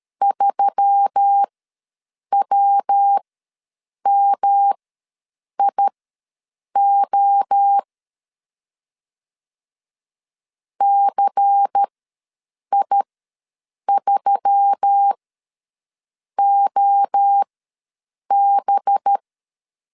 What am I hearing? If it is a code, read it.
Morse code, heard as a keyed tone, about 13 wpm, spaced as for 7 wpm, 800 Hz. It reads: 3WMIO CI3OB
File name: LEZIONI / 5 LBVC / 5 LBVC ws 12wpm 3159